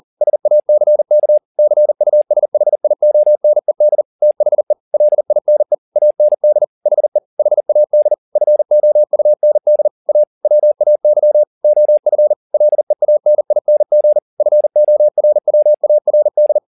Morse code, heard as a keyed tone, about 40 wpm, 600 Hz. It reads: SACK CUSHIONED THE LINE AND HE HAD FOUND A WAY OF LEANING FORWARD